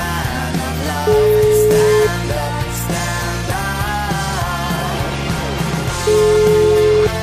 Ringback tone (Europe, 425 Hz), with noise.